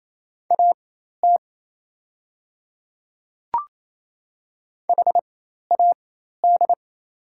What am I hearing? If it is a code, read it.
Morse code, heard as a keyed tone, about 28 wpm, spaced as for 13 wpm, 700 Hz. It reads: AT HAD
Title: 4x Character Spacing - Top 100 Words - 28wpm